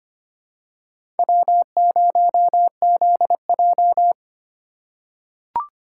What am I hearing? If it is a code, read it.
Morse code, heard as a keyed tone, about 25 wpm, 690 Hz. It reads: W0ZJ
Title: USA Call Signs - 4 Characters - 4x Character Spacing - 25wpm